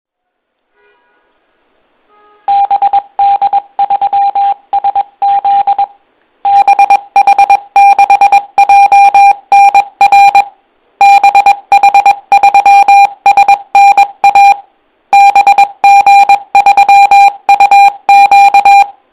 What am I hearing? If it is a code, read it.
Morse code, heard as a keyed tone, about 22 wpm, 800 Hz. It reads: BD3SZ BH6JNR BH3SNA BG3UQ